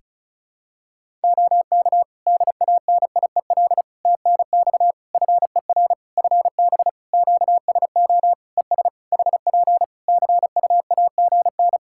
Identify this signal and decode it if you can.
Morse code, heard as a keyed tone, about 36 wpm, 700 Hz. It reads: OK DANIEL TNX FER FB QSO ES HP CUAGN
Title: K1USN SST QSO Practice 35wpm